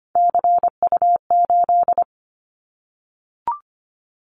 Morse code, heard as a keyed tone, about 25 wpm, 700 Hz. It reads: CU8
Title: Callsign Prefixes for Top 50 DXCC - 25wpm